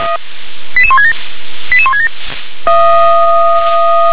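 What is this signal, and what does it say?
Five-tone selective call (ZVEI), heard as quick runs of five tones, tones of about 70 ms. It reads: 88167 88167
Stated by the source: ZVEI five-tone selective call received with an SDR.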